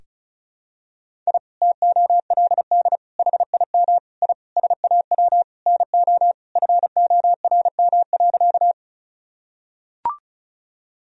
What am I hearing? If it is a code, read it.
Morse code, heard as a keyed tone, about 34 wpm, 700 Hz. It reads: I TOLD HIM I SAW NO FORM.
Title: Sentences from Top 1000 Words - Review Part 1 - 35wpm